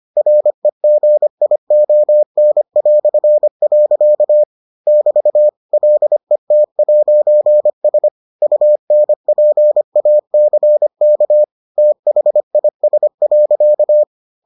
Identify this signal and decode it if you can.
Morse code, heard as a keyed tone, about 25 wpm, 600 Hz. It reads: REGION". <BT> LET'S UNPACK THIS.